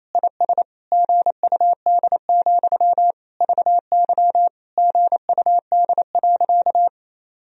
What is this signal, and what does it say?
Morse code, heard as a keyed tone, about 28 wpm, 700 Hz. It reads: IS GUD, VY GUD.
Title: Sentences from Top 200 Words and Abbreviations 28wpm